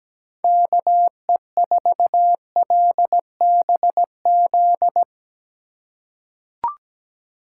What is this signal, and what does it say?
Morse code, heard as a keyed tone, about 17 wpm, 700 Hz. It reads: KE4LBZ
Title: USA Call Signs - 6 Characters - 4x Character Spacing - 17wpm